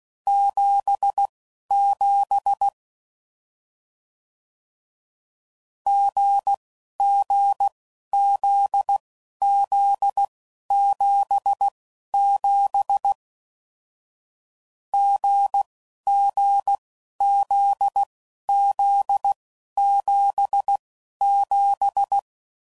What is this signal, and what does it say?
Morse code, heard as a keyed tone, about 16 wpm, spaced as for 11 wpm, 800 Hz. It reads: 77 GGZZ77 GGZZ77
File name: Mixed / Short Runs / GZ GZ7